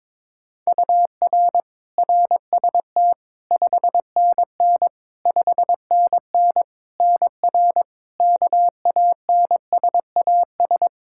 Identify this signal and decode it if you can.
Morse code, heard as a keyed tone, about 22 wpm, 700 Hz. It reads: UR RST 5NN 5NN NR KANSAS
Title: Practice QSO - Exchange 1 - Beginning - 22wpm